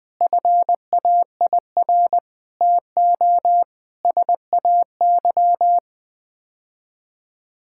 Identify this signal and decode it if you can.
Morse code, heard as a keyed tone, about 20 wpm, 700 Hz. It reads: FAIR TO SAY
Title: Sets of 6 Words - 20wpm